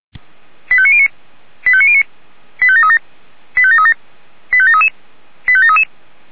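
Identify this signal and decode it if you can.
Five-tone selective call (ZVEI), heard as quick runs of five tones, tones of about 70 ms. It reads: 85909 85909 85737 85737 85733 85733
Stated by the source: Five-tone ZVEI selective call received on an SDR.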